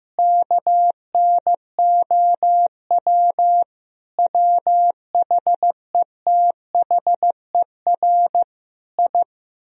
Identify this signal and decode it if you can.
Morse code, heard as a keyed tone, about 15 wpm, 700 Hz. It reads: KNOW WHETHER I
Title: Sentences from Top 1000 Words - Review Part 3 - 15wpm